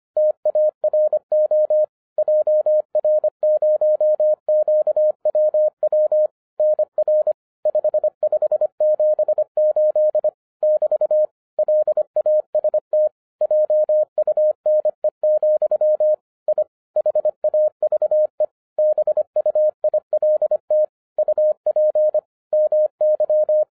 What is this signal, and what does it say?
Morse code, heard as a keyed tone, about 25 wpm, 610 Hz. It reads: TARO JR0QWW NR 5578 <BT> LAST JUNE, I HAVE BUILT UP MY